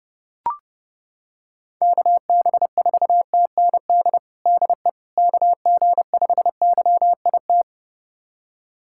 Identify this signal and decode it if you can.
Morse code, heard as a keyed tone, about 32 wpm, 700 Hz. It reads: KB4TND DE KG5YIT
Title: USA Callsign DE Callsign - 6 Characters - 30wpm